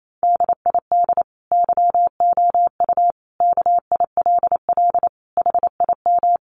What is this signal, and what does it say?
Morse code, heard as a keyed tone, about 28 wpm, 700 Hz. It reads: DID YOU KILL HIM
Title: Sentences from Top 1400 Words 28wpm